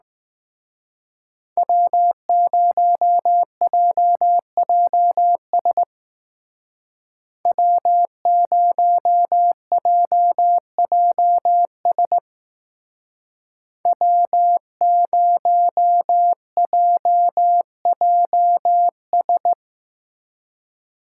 Morse code, heard as a keyed tone, about 18 wpm, 700 Hz. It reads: W0JJS W0JJS W0JJS
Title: Speed Racing - USA 1x3 Call Signs 15wpm